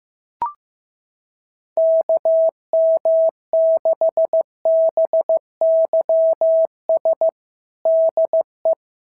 Morse code, heard as a keyed tone, about 14 wpm, 650 Hz. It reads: KM6BYS DE
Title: SOTA QSO Practice 15wpm